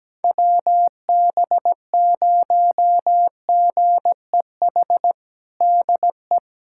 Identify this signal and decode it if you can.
Morse code, heard as a keyed tone, about 17 wpm, 700 Hz. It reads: WB0GEH DE
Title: USA Callsign DE Callsign 17wpm